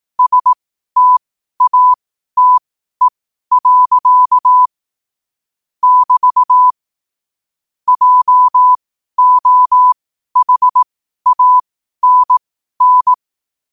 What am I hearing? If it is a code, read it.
Morse code, heard as a keyed tone, about 18 wpm, spaced as for 12 wpm, 1000 Hz. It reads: STATE. <BT> JOHANN
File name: qotdfiles / 12wpm